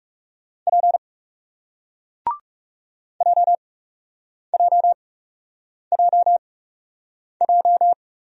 Morse code, heard as a keyed tone, about 41 wpm, 700 Hz. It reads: P J J J J